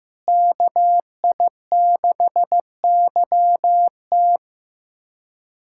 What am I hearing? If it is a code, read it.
Morse code, heard as a keyed tone, about 15 wpm, 700 Hz. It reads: KI6YT
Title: USA Callsign DE Callsign - 5 Characters - 15wpm